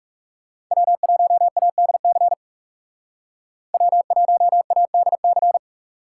Morse code, heard as a keyed tone, about 37 wpm, 700 Hz. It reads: W1ADC W1ADC
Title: Speed Racing - USA 1x3 Call Signs 30wpm